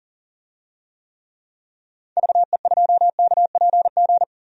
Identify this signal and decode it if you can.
Morse code, heard as a keyed tone, about 40 wpm, 700 Hz. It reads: VE2KPG
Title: DXCC-100 Call Signs - 6 Characters - 40wpm